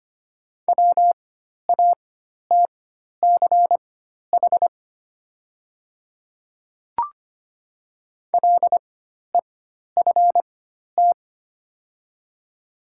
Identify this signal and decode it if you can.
Morse code, heard as a keyed tone, about 25 wpm, spaced as for 12 wpm, 700 Hz. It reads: WATCH LEFT
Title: 4x Character Spacing - Top 500 Words - 25wpm